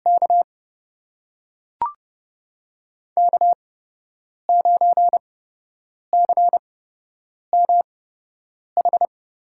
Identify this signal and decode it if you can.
Morse code, heard as a keyed tone, about 29 wpm, spaced as for 8 wpm, 700 Hz. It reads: K K9CMH
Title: USA Call Signs - 5 Characters - 8x Character Spacing - 30wpm